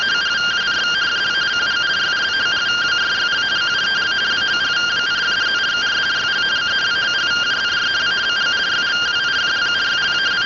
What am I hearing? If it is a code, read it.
Two-tone teleprinter signal, about 170 Hz shift.
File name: findings / CT 205 ax mips, iDose 3 0006.gen-08 rtty rx